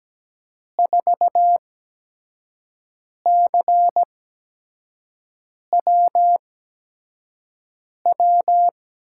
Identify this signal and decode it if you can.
Morse code, heard as a keyed tone, about 17 wpm, spaced as for 5 wpm, 700 Hz. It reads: 4CWW
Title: USA Call Signs - 5 Characters - 8x Character Spacing - 17wpm